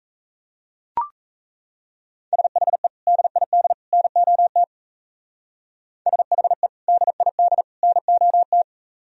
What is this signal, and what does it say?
Morse code, heard as a keyed tone, about 34 wpm, 700 Hz. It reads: SHE DID NOT SHE DID NOT
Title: Speed-Racing - Sets of 3 Words 28wpm